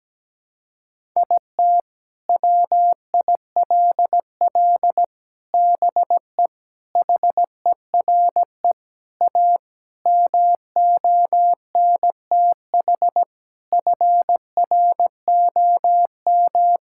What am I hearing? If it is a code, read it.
Morse code, heard as a keyed tone, about 17 wpm, 700 Hz. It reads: IT WILL BE HERE A MONTH FROM
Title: Sentences from Top 1000 Words - Review Part 5 - 17wpm